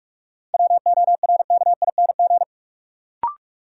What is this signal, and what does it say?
Morse code, heard as a keyed tone, about 46 wpm, 700 Hz. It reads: WORKING
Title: Sentences from Top 700 Words - New Word Drill - 45wpm